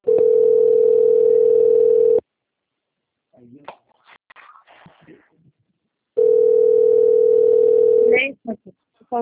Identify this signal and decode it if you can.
Ringback tone (North America), 440 + 480 Hz, with speech.
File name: out- - - - - .8093